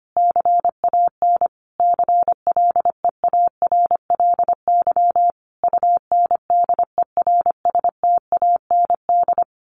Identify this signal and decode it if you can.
Morse code, heard as a keyed tone, about 25 wpm, 700 Hz. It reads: CAN CLEARLY UNDERSTAND
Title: Sentences from Top 1400 Words 25wpm